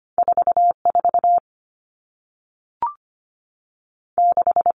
Morse code, heard as a keyed tone, about 25 wpm, spaced as for 3 wpm, 700 Hz. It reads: 4 46
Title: Numbers - 2-Digit 25wpm